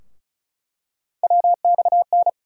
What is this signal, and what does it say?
Morse code, heard as a keyed tone, about 34 wpm, 700 Hz. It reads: WXN
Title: USA Call Sign Postfixes - 3 Letters 35wpm